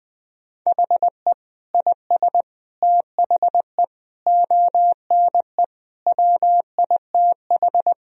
Morse code, heard as a keyed tone, about 20 wpm, 700 Hz. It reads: HE IS THE ONE WITH
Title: Sentences from Top 1000 Words - Review Part 3 - 20wpm